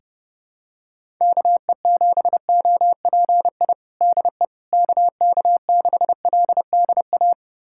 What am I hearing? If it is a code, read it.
Morse code, heard as a keyed tone, about 30 wpm, 700 Hz. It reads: KE7OPI DE KK6LDA
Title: USA Callsign DE Callsign - 6 Characters - 30wpm